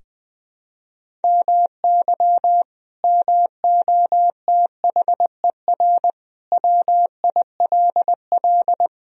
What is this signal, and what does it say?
Morse code, heard as a keyed tone, about 21 wpm, 700 Hz. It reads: MY MOTHER WILL
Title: Sentences from Top 1000 Words - Review Part 1 - 20wpm